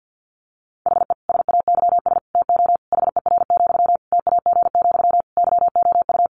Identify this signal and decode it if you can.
Morse code, heard as a keyed tone, about 50 wpm, 700 Hz. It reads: HE SAYS TO HER, TAG, YOU
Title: Additional Easy Phrases 50wpm